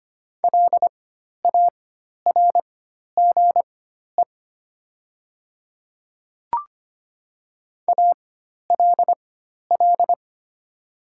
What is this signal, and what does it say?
Morse code, heard as a keyed tone, about 25 wpm, spaced as for 12 wpm, 690 Hz. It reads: LARGE ALL
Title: 4x Character Spacing - Top 500 Words - 25wpm